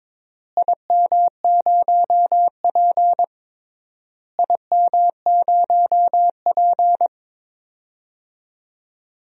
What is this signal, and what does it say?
Morse code, heard as a keyed tone, about 21 wpm, 700 Hz. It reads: IM0P IM0P
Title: DXCC-100 Call Signs - 4 Characters - Repeated 3x - 22wpm